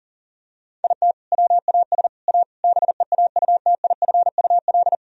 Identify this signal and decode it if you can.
Morse code, heard as a keyed tone, about 40 wpm, 700 Hz. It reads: IT WAS A BEAUTIFUL
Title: Sentences from Top 1000 Words - Review Part 3 - 40wpm